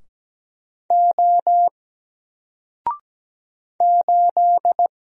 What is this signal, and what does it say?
Morse code, heard as a keyed tone, about 17 wpm, 700 Hz. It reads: O 8
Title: Single Character (Letter:Number:Punctuation:Prosign) Alt 17wpm